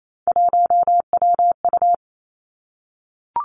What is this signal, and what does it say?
Morse code, heard as a keyed tone, about 28 wpm, 700 Hz. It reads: 1WU
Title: Three Challenging Letter-Number Combinations 28wpm